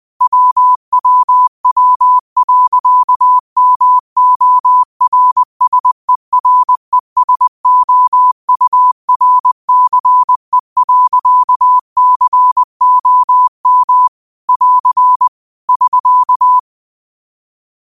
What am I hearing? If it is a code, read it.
Morse code, heard as a keyed tone, about 20 wpm, 1000 Hz. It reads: WWW.MORSERESOURCE.COM <AR> <SK>